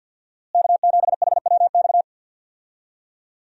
Morse code, heard as a keyed tone, about 46 wpm, 700 Hz. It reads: K7HWX
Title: USA Call Signs - 5 Characters - Repeated 3x - 50wpm